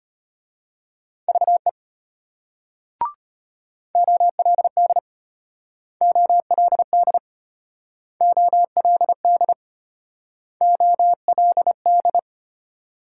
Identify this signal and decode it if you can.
Morse code, heard as a keyed tone, about 32 wpm, 700 Hz. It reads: VE OLD OLD OLD OLD